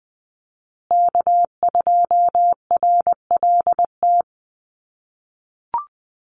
Morse code, heard as a keyed tone, about 20 wpm, 700 Hz. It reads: K2RLT
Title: USA Call Signs Repeated 2x 20wpm